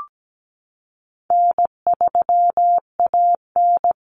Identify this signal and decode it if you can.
Morse code, heard as a keyed tone, about 17 wpm, 700 Hz. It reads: N3AN
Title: Common Cut Numbers - 4-Digit 17wpm